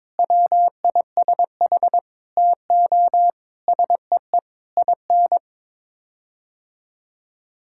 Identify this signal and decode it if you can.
Morse code, heard as a keyed tone, about 22 wpm, 700 Hz. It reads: WISH TO SEE IN
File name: practice / 22wpm / Sets of 8 Words - 22wpm